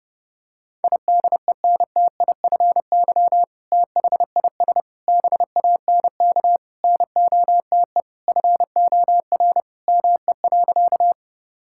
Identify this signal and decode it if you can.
Morse code, heard as a keyed tone, about 30 wpm, 700 Hz. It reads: IDENTIFY THIS BANK NOTE FOR ME.
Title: Sentences from Top 1000 Words - Review Part 6 - 30wpm